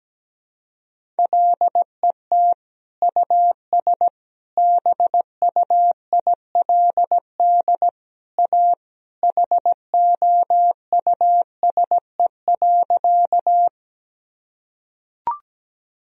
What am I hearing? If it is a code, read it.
Morse code, heard as a keyed tone, about 17 wpm, 700 Hz. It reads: LET US BUILD A HOUSE.
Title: Sentences from Top 1000 Words - Review Part 1 - 17wpm